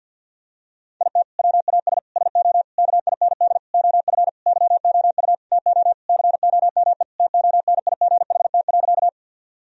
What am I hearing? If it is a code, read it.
Morse code, heard as a keyed tone, about 50 wpm, 700 Hz. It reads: IT WAS SO KIND OF YOU TO COME TONIGHT.
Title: Sentences from Top 1000 Words - Review Part 3 - 50wpm